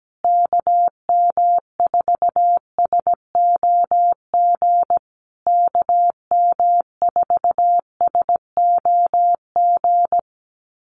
Morse code, heard as a keyed tone, about 17 wpm, 700 Hz. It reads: KM4SOG KM4SOG
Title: USA Call Signs Repeated 3x 17wpm